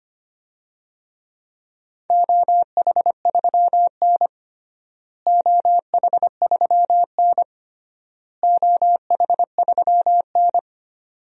Morse code, heard as a keyed tone, about 25 wpm, 700 Hz. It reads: OH3N OH3N OH3N
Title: DXCC-100 Call Signs - 4 Characters - Repeated 3x - 25wpm